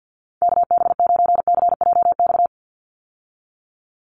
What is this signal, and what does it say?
Morse code, heard as a keyed tone, about 50 wpm, 700 Hz. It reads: KB9CJX
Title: USA Call Signs 50wpm